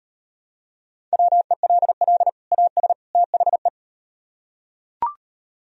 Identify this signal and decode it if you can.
Morse code, heard as a keyed tone, about 38 wpm, 700 Hz. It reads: WELL AS THE